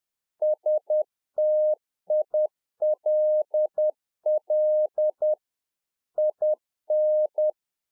Morse code, heard as a keyed tone, about 10 wpm, 610 Hz. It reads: STILL IN
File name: Czech Army Morse Keys 10WPM